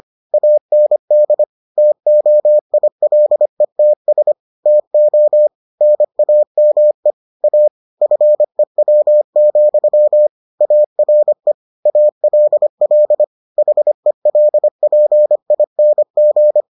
Morse code, heard as a keyed tone, about 25 wpm, 600 Hz. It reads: AND TOILETS TO NAME A FEW, ARE ALL HELPING